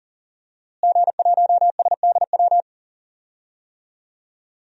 Morse code, heard as a keyed tone, about 40 wpm, 700 Hz. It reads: G1SDW
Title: DXCC-100 Call Signs - 5 Characters - Repeated 3x - 40wpm